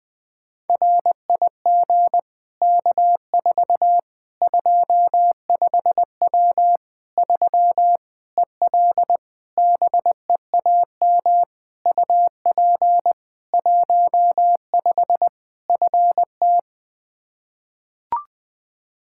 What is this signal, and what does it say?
Morse code, heard as a keyed tone, about 20 wpm, 700 Hz. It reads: RIG K4 25W 3 EL BEAM UP 15 FT
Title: Practice QSO - Exchange 2 - Info - 20wpm